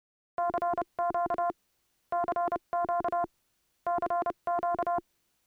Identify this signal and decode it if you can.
Morse code, heard as a keyed tone, about 29 wpm, spaced as for 25 wpm, 700 Hz, with noise. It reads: CQ CQ CQ